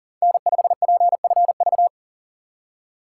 Morse code, heard as a keyed tone, about 40 wpm, 700 Hz. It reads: N5PFV
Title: USA Call Signs - 2x Character Spacing - 40wpm